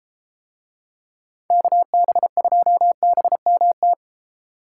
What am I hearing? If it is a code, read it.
Morse code, heard as a keyed tone, about 32 wpm, 700 Hz. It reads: KB2BMT